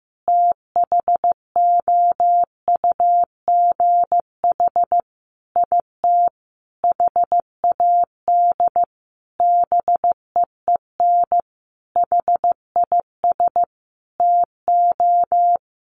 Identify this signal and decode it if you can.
Morse code, heard as a keyed tone, about 15 wpm, 700 Hz. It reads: THOUGH IT HAD BEEN HIS TO
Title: Sentences from Top 1500 Words 15wpm